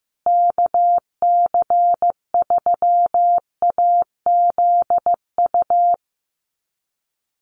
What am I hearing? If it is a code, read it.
Morse code, heard as a keyed tone, about 15 wpm, 700 Hz. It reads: KC3AZU
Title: USA Call Signs Repeated 2x 15wpm